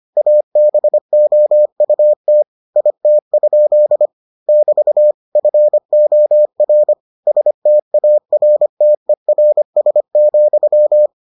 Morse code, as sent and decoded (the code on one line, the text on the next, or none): .- -... --- ..- - / .. - ..--.. / -...- / ..-. --- .-. / ... - .- .-. - . .-. ... --..--
ABOUT IT? <BT> FOR STARTERS,